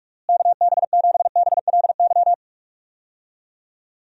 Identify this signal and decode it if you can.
Morse code, heard as a keyed tone, about 46 wpm, 700 Hz. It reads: KB7BLY
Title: USA Call Signs - 6 Characters - Repeated 3x - 45wpm